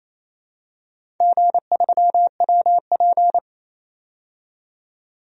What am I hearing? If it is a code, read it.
Morse code, heard as a keyed tone, about 28 wpm, 700 Hz. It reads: G3WP
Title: DXCC-100 Call Signs - 4 Characters - 28wpm